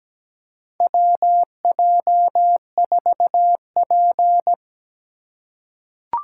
Morse code, heard as a keyed tone, about 17 wpm, 700 Hz. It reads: WJ4P
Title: USA Call Signs - 4 Characters - 3x Character Spacing - 17wpm